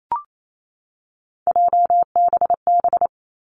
Morse code, heard as a keyed tone, about 28 wpm, 700 Hz. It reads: JBB
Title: Three Challenging Letter-Number Combinations 28wpm